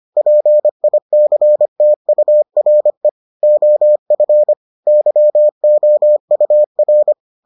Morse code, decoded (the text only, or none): PICTURE OF YOUR